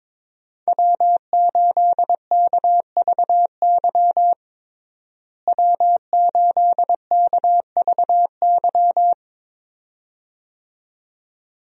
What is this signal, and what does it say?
Morse code, heard as a keyed tone, about 22 wpm, 700 Hz. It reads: W8KVY W8KVY
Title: USA Call Signs - 5 Characters - Repeated 2x - 22wpm